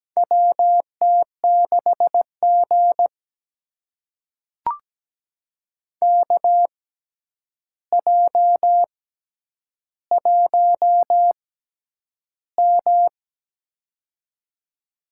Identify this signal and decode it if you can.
Morse code, heard as a keyed tone, about 17 wpm, spaced as for 9 wpm, 700 Hz. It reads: WT6G KJ1M
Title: USA Call Signs - 4 Characters - 6x Character Spacing - 17wpm